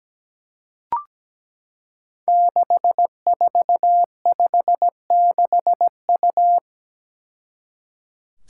Morse code, heard as a keyed tone, about 17 wpm, 700 Hz. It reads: 6456U
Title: Five Challenging Letter-Number Combinations 17wpm